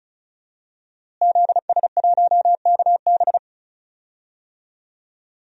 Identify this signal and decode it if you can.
Morse code, heard as a keyed tone, about 36 wpm, 700 Hz. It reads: ZS1KB
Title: DXCC-100 Call Signs - 5 Characters - Repeated 2x - 35wpm